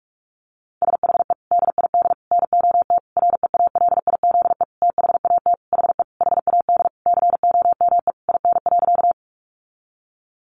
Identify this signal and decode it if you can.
Morse code, heard as a keyed tone, about 46 wpm, 700 Hz. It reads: SHE DID NOT REALIZE THAT HE HAD COME IN.
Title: Sentences from Top 1000 Words 45wpm